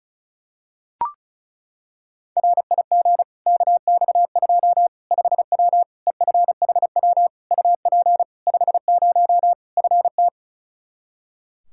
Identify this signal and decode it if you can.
Morse code, heard as a keyed tone, about 34 wpm, 700 Hz. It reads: RIG KX2 5W EFHW UP 50 FT
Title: Practice QSO - Exchange 2 - Info - 35wpm